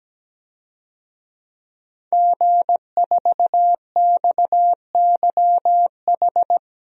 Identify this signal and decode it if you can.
Morse code, heard as a keyed tone, about 17 wpm, 700 Hz. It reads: G4XYH
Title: DXCC-100 Call Signs - 5 Characters - 17wpm